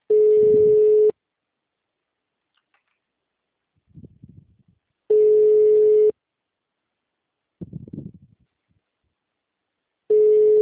Ringback tone (Europe, 425 Hz).
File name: out- -2011- - - .17